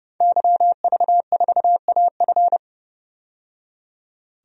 Morse code, heard as a keyed tone, about 30 wpm, 700 Hz. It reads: YV4AF
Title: DXCC-100 Call Signs - 5 Characters - 30wpm